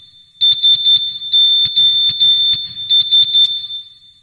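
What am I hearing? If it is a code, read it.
Morse code, heard as a keyed tone, about 11 wpm, 3550 Hz. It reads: SOS
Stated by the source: Morse SOS ringtone from MeloBoom.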